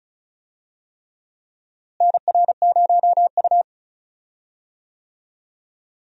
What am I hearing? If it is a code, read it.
Morse code, heard as a keyed tone, about 36 wpm, 700 Hz. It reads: NR0U
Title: USA Call Signs - 4 Characters - Repeated 3x - 35wpm